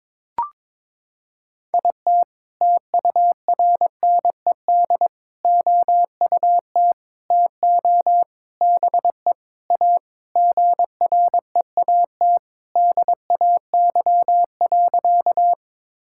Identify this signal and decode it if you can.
Morse code, heard as a keyed tone, about 22 wpm, 700 Hz. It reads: IT TURNED OUT TO BE A GREAT DAY.
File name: practice / 22wpm / Sentences with 4-Word Phrases - 22wpm